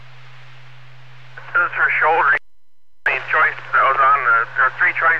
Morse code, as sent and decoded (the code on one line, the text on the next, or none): none